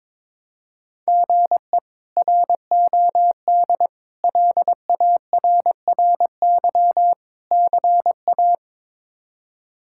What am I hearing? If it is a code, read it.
Morse code, heard as a keyed tone, about 22 wpm, 700 Hz. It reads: GE ROD LARRY CA
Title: K1USN SST QSO Practice - 22wpm